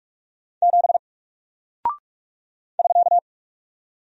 Morse code, heard as a keyed tone, about 46 wpm, 700 Hz. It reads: 7 <SK>
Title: Single Character (Letter:Number:Punctuation:Prosign) 45wpm